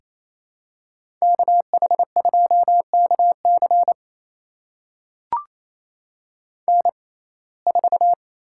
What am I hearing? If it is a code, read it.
Morse code, heard as a keyed tone, about 28 wpm, spaced as for 20 wpm, 700 Hz. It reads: KH2KC N4
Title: USA Call Signs - 5 Characters - 6x Character Spacing - 28wpm